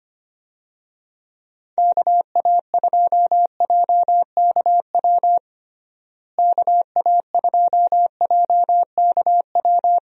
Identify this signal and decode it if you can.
Morse code, heard as a keyed tone, about 25 wpm, 700 Hz. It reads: KA2JKW KA2JKW
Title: USA Call Signs - 6 Characters - Repeated 3x - 25wpm